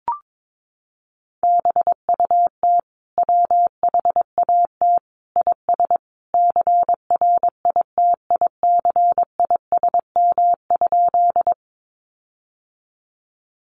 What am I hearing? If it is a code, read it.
Morse code, heard as a keyed tone, about 22 wpm, 700 Hz. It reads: BUT WHAT IS CRITICISM?
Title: Sentences from Top 1300 Words 22wpm